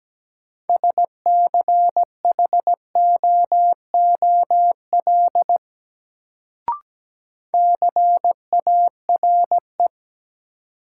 Morse code, heard as a keyed tone, about 17 wpm, 700 Hz. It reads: SCHOOL CARE
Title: Sentences from Top 300 Words - New Word Drill - 17wpm